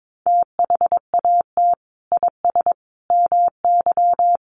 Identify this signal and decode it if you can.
Morse code, heard as a keyed tone, about 22 wpm, 700 Hz. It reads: THAT IS MY
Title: Sentences from Top 1500 Words 22wpm